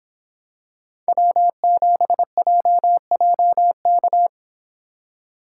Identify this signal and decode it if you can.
Morse code, heard as a keyed tone, about 26 wpm, 700 Hz. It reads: W7JJK